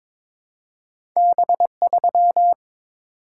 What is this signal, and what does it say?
Morse code, heard as a keyed tone, about 22 wpm, 700 Hz. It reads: B3
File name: practice / 22wpm / Two Letter-Number Combinations - 22wpm